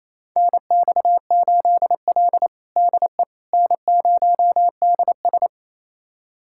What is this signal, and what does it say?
Morse code, heard as a keyed tone, about 27 wpm, 700 Hz. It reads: NX8L DE N0DS
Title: USA Callsign DE Callsign - 4 Characters - 28wpm